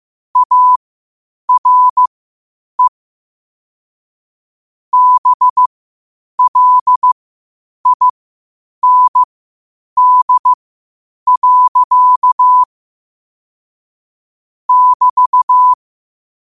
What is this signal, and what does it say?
Morse code, heard as a keyed tone, about 15 wpm, spaced as for 8 wpm, 1000 Hz. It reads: ARE BLIND. <BT>
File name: qotdfiles / 8wpm